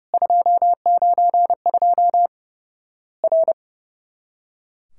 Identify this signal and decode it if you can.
Morse code, heard as a keyed tone, about 30 wpm, 700 Hz. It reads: 292 R
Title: Worked All Europe DX Contest - QTC Exchange - 30wpm